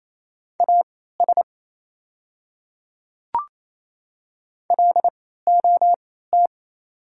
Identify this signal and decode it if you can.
Morse code, heard as a keyed tone, about 27 wpm, spaced as for 16 wpm, 700 Hz. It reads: AS LOT
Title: 3x Character Spacing - Top 500 Words - 28wpm